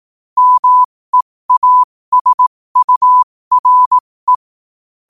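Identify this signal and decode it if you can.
Morse code, heard as a keyed tone, about 18 wpm, spaced as for 15 wpm, 1000 Hz. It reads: MEASURE
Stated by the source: Morse quote-of-the-day practice at 15 WPM from Morse Resource.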